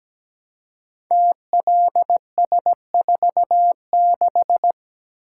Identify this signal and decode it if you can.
Morse code, heard as a keyed tone, about 17 wpm, 700 Hz. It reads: TLS46
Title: Code Groups - 5 Letter:Numbers - 17wpm